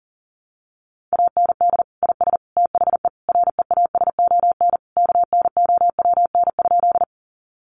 Morse code, heard as a keyed tone, about 40 wpm, 700 Hz. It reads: AND IS THE REASON KNOWN?
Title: Sentences from Top 700 Words 40wpm